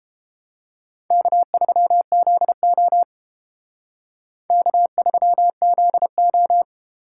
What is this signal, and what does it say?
Morse code, heard as a keyed tone, about 30 wpm, 700 Hz. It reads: K3ZO K3ZO